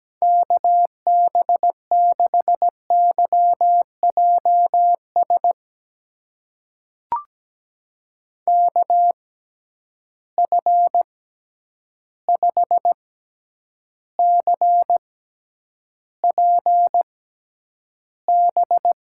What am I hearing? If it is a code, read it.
Morse code, heard as a keyed tone, about 17 wpm, spaced as for 9 wpm, 700 Hz. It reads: KB6YJS KF5CPB
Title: USA Call Signs - 6x Character Spacing - 17wpm